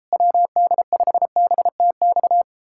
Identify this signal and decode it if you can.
Morse code, heard as a keyed tone, about 38 wpm, 700 Hz. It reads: WD5BTX